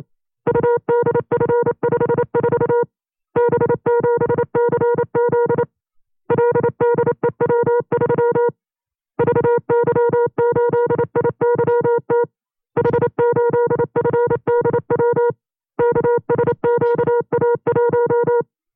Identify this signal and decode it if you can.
Morse code, heard as a keyed tone, about 29 wpm, 480 Hz. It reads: UDF54 B7CZ LDEW3 VY8IYT H8FDW KSQA1
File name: FreeFiles / CW-practice / 25WPM-side1